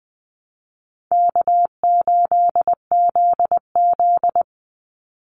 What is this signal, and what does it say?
Morse code, heard as a keyed tone, about 20 wpm, 700 Hz. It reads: K8ZZ
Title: Easy Call Signs Repeated 2x - 20wpm